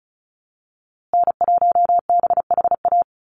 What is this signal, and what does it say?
Morse code, heard as a keyed tone, about 34 wpm, 700 Hz. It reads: N1BHA
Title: USA Call Signs Repeated 3x 35wpm